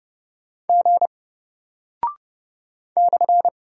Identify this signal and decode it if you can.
Morse code, heard as a keyed tone, about 30 wpm, 700 Hz. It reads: G /
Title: Single Character (Letter:Number:Punctuation:Prosign) 30wpm